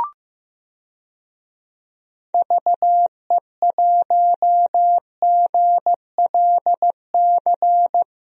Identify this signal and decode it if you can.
Morse code, heard as a keyed tone, about 15 wpm, 700 Hz. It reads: VE1GLC
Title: DXCC-100 Call Signs - 6 Characters - Repeated 2x - 15wpm